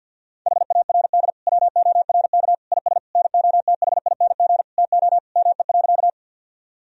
Morse code, heard as a keyed tone, about 52 wpm, 700 Hz. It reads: HARD WORK IS NOTHING TO ME.
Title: Sentences from Top 1000 Words - Review Part 1 - 50wpm